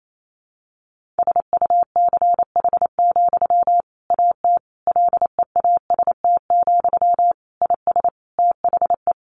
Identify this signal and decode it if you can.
Morse code, heard as a keyed tone, about 28 wpm, 700 Hz. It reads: SUCH, AT LEAST, IS THE
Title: Sentences from Top 1500 Words 28wpm